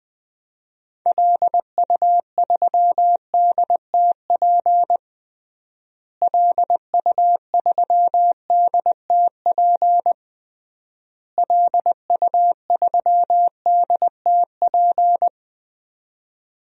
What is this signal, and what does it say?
Morse code, heard as a keyed tone, about 20 wpm, 700 Hz. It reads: LU3DTP LU3DTP LU3DTP
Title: DXCC-100 Call Signs - 6 Characters - Repeated 3x - 20wpm